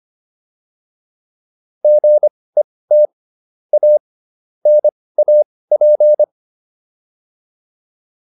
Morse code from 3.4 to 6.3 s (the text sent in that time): A NAP